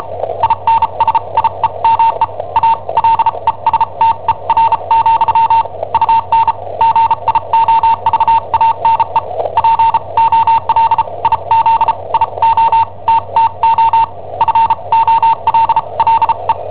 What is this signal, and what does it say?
Morse code, heard as a keyed tone, about 34 wpm, spaced as for 28 wpm, 910 Hz. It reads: INSIEME A LESTER, UN GIOVANE POLIZIOTTO FOLLE
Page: Cassetta esercizi preparata da I7OHP per IW5ELP